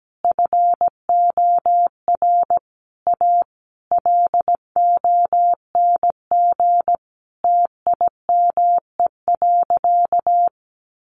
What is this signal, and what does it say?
Morse code, heard as a keyed tone, about 17 wpm, 700 Hz. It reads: FOR A LONG TIME.
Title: Sentences from Top 1000 Words 17wpm